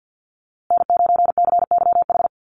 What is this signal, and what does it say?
Morse code, heard as a keyed tone, about 48 wpm, 700 Hz. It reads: N9CYH
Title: USA Call Signs Repeated 3x 50wpm